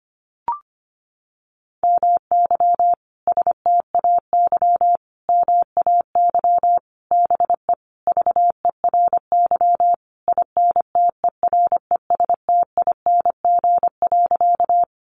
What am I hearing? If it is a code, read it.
Morse code, heard as a keyed tone, about 25 wpm, 700 Hz. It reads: MY STAY MAY BE VERY INTERESTING.
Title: Sentences from Top 1000 Words 25wpm